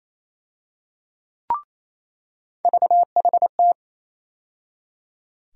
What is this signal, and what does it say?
Morse code, heard as a keyed tone, about 28 wpm, 700 Hz. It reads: VHT
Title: USA Call Sign Postfixes - 3 Letters 28wpm